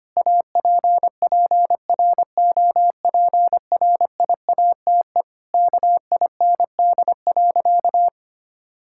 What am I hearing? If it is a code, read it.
Morse code, heard as a keyed tone, about 25 wpm, 700 Hz. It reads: APPROPRIATE KIND.
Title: Sentences from Top 1000 Words - Review Part 5 - 25wpm